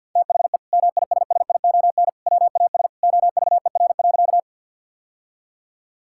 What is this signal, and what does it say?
Morse code, heard as a keyed tone, about 50 wpm, 700 Hz. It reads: THE MISSION WAS OVER.
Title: Sentences from Top 1600 Words 50wpm